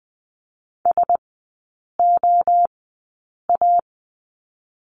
Morse code, heard as a keyed tone, about 20 wpm, spaced as for 8 wpm, 700 Hz. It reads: SOA
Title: 035 Lesson 7e - Letter s - Three Characters 20wpm